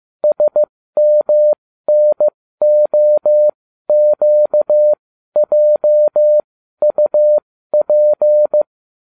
Morse code, heard as a keyed tone, about 15 wpm, spaced as for 12 wpm, 600 Hz. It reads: SMNOQJUP